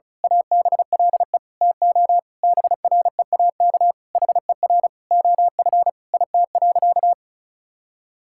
Morse code, heard as a keyed tone, about 36 wpm, 700 Hz. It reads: ABLE TO BREAK HER OF IT.
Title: Sentences from Top 1000 Words - Review Part 6 - 35wpm